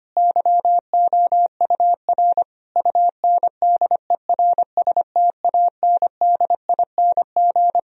Morse code, heard as a keyed tone, about 25 wpm, 700 Hz. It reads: YOUR UNDERSTANDING
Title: Sentences from Top 1000 Words - Review Part 3 - 25wpm